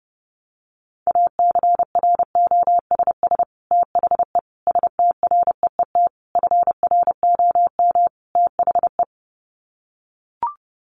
Morse code, heard as a keyed tone, about 30 wpm, 700 Hz. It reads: ACROSS THE STREET FROM THE
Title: Sets of 5 Words Encore 30wpm